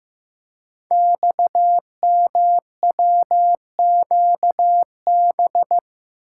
Morse code, heard as a keyed tone, about 15 wpm, 700 Hz. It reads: XMWQB
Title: Code Groups - 5 Letter:Numbers - 15wpm